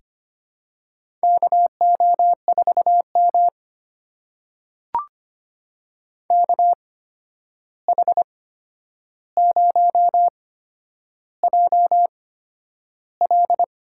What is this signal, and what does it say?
Morse code, heard as a keyed tone, about 25 wpm, spaced as for 10 wpm, 700 Hz. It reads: KO4M KH0JL
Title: USA Call Signs - 8x Character Spacing - 25wpm